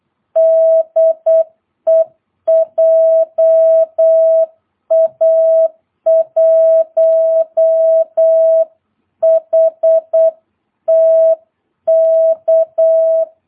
Morse code, heard as a keyed tone, about 8 wpm, 650 Hz. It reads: DE JA1HTK
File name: jh8cbh / 2022cw